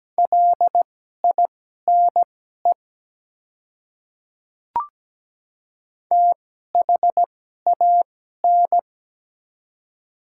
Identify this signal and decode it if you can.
Morse code, heard as a keyed tone, about 17 wpm, spaced as for 12 wpm, 700 Hz. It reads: LINE THAN
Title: 2x Character Spacing - Top 200 Words - 17wpm